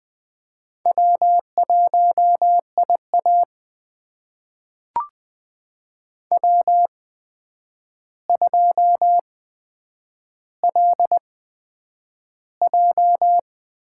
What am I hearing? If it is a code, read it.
Morse code, heard as a keyed tone, about 20 wpm, spaced as for 9 wpm, 700 Hz. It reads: W1IA W2LJ
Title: USA Call Signs - 4 Characters - 8x Character Spacing - 20wpm